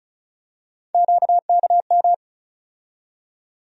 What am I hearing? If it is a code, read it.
Morse code, heard as a keyed tone, about 34 wpm, 700 Hz. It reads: QKM
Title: USA Call Sign Postfixes - 3 Letters 35wpm